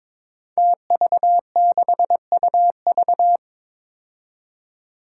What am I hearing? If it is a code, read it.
Morse code, heard as a keyed tone, about 23 wpm, 700 Hz. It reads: TV6UV
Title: Five Challenging Letter-Number Combinations 22wpm